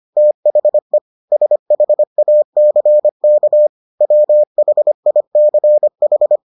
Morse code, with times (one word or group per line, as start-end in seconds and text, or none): 0.2-1.0 THE
1.3-3.7 SHACK
4.0-6.4 WHICH